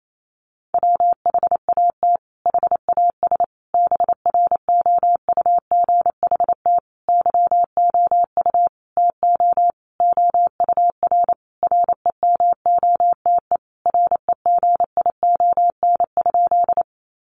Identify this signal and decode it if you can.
Morse code, heard as a keyed tone, about 28 wpm, 700 Hz. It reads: WHAT HAS BROUGHT YOU TO OUR REMOTE REGION?
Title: Sentences from Top 1500 Words 28wpm